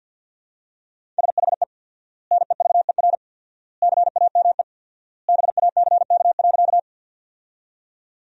Morse code, heard as a keyed tone, about 52 wpm, 700 Hz. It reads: SHE NEVER CAME BACK.
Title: 4x Word-Spacing - Sentences from Top 300 Words - 50wpm